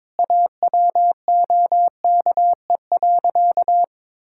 Morse code, heard as a keyed tone, about 22 wpm, 700 Hz. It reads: AWOKE.
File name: practice / 22wpm / Sentences with 4-Word Phrases - 22wpm